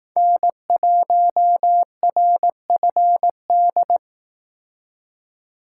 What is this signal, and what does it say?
Morse code, heard as a keyed tone, about 18 wpm, 700 Hz. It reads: N1RFD
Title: Speed Racing - USA Call Signs 15wpm